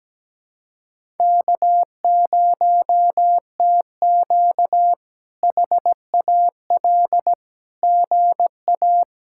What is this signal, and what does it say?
Morse code, heard as a keyed tone, about 17 wpm, 700 Hz. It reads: K0TQ HAL GA
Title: K1USN SST QSO Practice - 17wpm